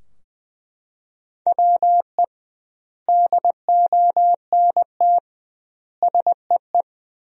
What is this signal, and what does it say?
Morse code, heard as a keyed tone, about 21 wpm, 700 Hz. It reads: WE DONT SEE
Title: 2x Word-Spacing - Sentences from Top 300 Words - 20wpm